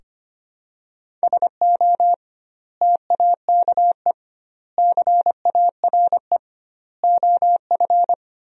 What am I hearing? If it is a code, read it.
Morse code, heard as a keyed tone, about 25 wpm, 700 Hz. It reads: SO TAKE CARE OF
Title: 2x Word-Spacing - Sentences from Top 300 Words - 25wpm